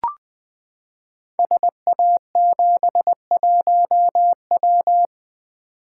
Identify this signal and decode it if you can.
Morse code, heard as a keyed tone, about 20 wpm, 700 Hz. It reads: SA71W
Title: Code Groups - 5 Letter:Numbers - 20wpm